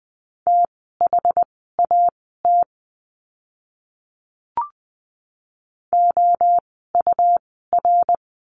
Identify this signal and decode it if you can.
Morse code, heard as a keyed tone, about 20 wpm, spaced as for 14 wpm, 700 Hz. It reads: THAT OUR
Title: 192 Lesson 31f - Letter v - Words 20wpm